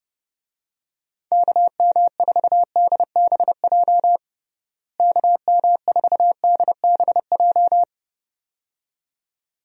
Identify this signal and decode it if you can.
Morse code, heard as a keyed tone, about 30 wpm, 700 Hz. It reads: KM4DBJ KM4DBJ
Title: USA Call Signs - 6 Characters - Repeated 2x - 30wpm